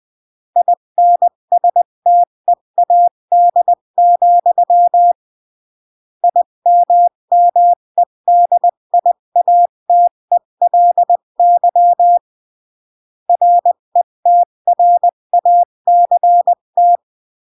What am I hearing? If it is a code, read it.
Morse code, heard as a keyed tone, about 20 wpm, 700 Hz. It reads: INSTEAD, IMMEDIATELY RETRACT
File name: CW / jodorowsky / 82 commandments 20-15